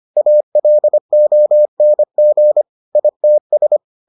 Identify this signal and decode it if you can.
Morse code, heard as a keyed tone, about 25 wpm, 600 Hz. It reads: ALONG ITS